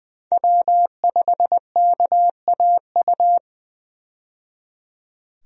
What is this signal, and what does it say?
Morse code, heard as a keyed tone, about 20 wpm, 700 Hz. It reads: W5KAU
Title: USA Call Signs - 5 Characters - 20wpm